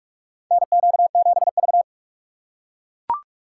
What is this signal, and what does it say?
Morse code, heard as a keyed tone, about 44 wpm, 700 Hz. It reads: NQ7V
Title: USA Call Signs - 4 Characters - 4x Character Spacing - 45wpm